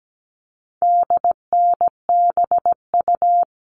Morse code, heard as a keyed tone, about 17 wpm, 700 Hz. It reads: DNBU
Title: All Cut Numbers - 4-Digit 17wpm